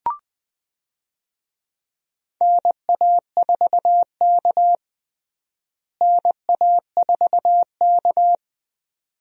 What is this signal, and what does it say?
Morse code, heard as a keyed tone, about 20 wpm, 700 Hz. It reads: NA4K NA4K
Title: USA Call Signs - 4 Characters - Repeated 2x - 20wpm